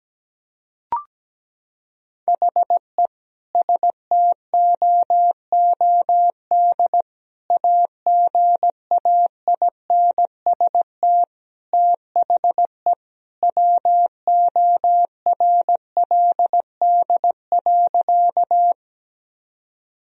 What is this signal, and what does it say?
Morse code, heard as a keyed tone, about 17 wpm, 700 Hz. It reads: HE STOOD AGAINST THE WORLD.
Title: Sentences from Top 1000 Words - Review Part 2 - 17wpm